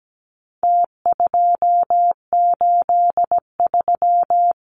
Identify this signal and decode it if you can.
Morse code, heard as a keyed tone, about 17 wpm, 700 Hz. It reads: T283
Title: Numbers - Contest Style - 4-Digit 17wpm